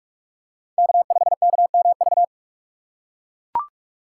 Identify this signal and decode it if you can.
Morse code, heard as a keyed tone, about 41 wpm, 700 Hz. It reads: K5KMV
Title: USA Call Signs - 5 Characters - 2x Character Spacing - 45wpm